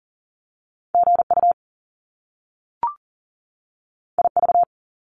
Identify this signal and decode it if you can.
Morse code, heard as a keyed tone, about 40 wpm, 700 Hz. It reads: GU IV
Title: Top Two Letter Combinations in Words 40wpm